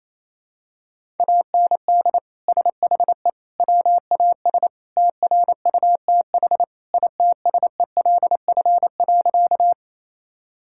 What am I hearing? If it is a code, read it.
Morse code, heard as a keyed tone, about 28 wpm, 700 Hz. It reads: AND SHE WAS TRUTH ITSELF.
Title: Sentences from Top 1000 Words - Review Part 5 - 28wpm